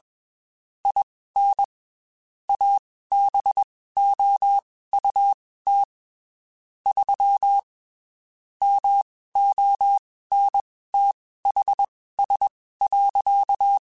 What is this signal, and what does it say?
Morse code, heard as a keyed tone, about 21 wpm, spaced as for 15 wpm, 800 Hz. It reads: IN ABOUT 3 MONTHS.
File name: QSOs / QSOs-mp3s / WD0BGP 1@20